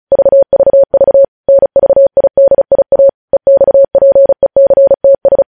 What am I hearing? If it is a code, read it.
Morse code, heard as a keyed tone, about 36 wpm, 550 Hz. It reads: VVV NVIDIA EXPECTS
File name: podcast / News headlines-1x-35x35